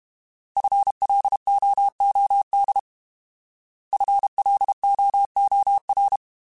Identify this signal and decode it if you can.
Morse code, heard as a keyed tone, about 32 wpm, 800 Hz. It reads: FLOOD FLOOR